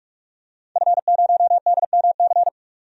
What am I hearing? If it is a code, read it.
Morse code, heard as a keyed tone, about 45 wpm, 700 Hz. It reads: F0DMC
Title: DXCC-100 Call Signs - 5 Characters - 45wpm